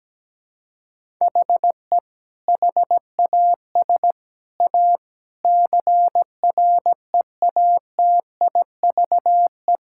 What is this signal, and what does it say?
Morse code, heard as a keyed tone, about 17 wpm, 700 Hz. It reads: HE HAS A CREATIVE
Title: Sentences from Top 1000 Words - Review Part 5 - 17wpm